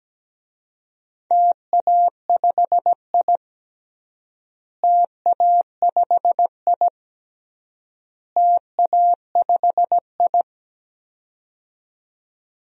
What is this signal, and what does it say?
Morse code, heard as a keyed tone, about 17 wpm, 700 Hz. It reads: TA5I TA5I TA5I
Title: DXCC-100 Call Signs - 4 Characters - Repeated 3x - 17wpm